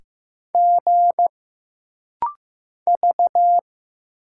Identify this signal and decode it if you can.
Morse code, heard as a keyed tone, about 15 wpm, 700 Hz. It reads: G V
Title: Single Letters - Rapid-Fire 15wpm